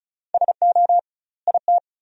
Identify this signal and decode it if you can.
Morse code, heard as a keyed tone, about 34 wpm, 700 Hz. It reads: SO IT
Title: 2x Word-Spacing - Sentences from Top 300 Words - 35wpm